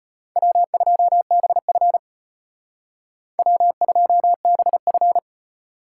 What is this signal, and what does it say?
Morse code, heard as a keyed tone, about 32 wpm, 700 Hz. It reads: W2BF W2BF